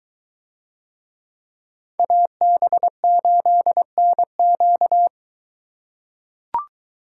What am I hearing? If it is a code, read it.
Morse code, heard as a keyed tone, about 22 wpm, 700 Hz. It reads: AB8NQ